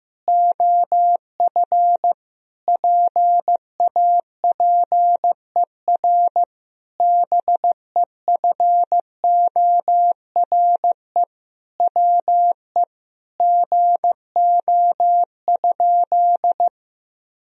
Morse code, heard as a keyed tone, about 15 wpm, 700 Hz. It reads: OF PAPER BEFORE WE GO?
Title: Sentences from Top 1000 Words - Review Part 2 - 15wpm